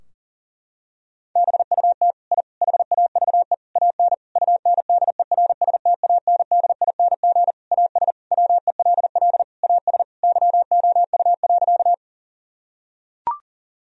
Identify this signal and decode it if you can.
Morse code, heard as a keyed tone, about 40 wpm, 700 Hz. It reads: BUT I HAVE AN UNDERSTANDING AS WELL AS YOU.
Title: Sentences from Top 1000 Words - Review Part 3 - 40wpm